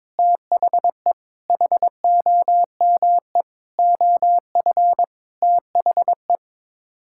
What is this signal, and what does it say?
Morse code, heard as a keyed tone, about 22 wpm, 700 Hz. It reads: THE HOME OF THE
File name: Sets of 9 Words - 22wpm